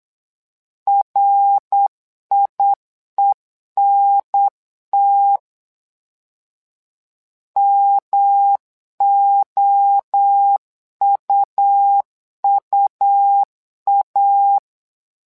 Morse code, heard as a keyed tone, about 8 wpm, 800 Hz. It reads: RIENT MOUUA
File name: LEZIONI / 3 MWRO / 3 MWRO 8wpm 2606